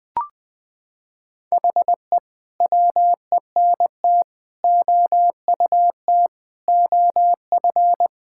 Morse code, heard as a keyed tone, about 20 wpm, 700 Hz. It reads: HE WENT OUT OF
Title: Sentences with 6-Word Phrases - 20wpm